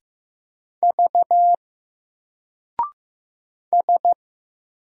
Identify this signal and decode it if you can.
Morse code, heard as a keyed tone, about 15 wpm, 700 Hz. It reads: V S
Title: Single Character (Letter:Number:Punctuation:Prosign) 15wpm